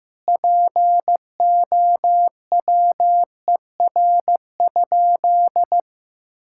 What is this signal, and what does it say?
Morse code, heard as a keyed tone, about 14 wpm, 700 Hz. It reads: POWER?
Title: Sentences from Top 1000 Words - Review Part 5 - 15wpm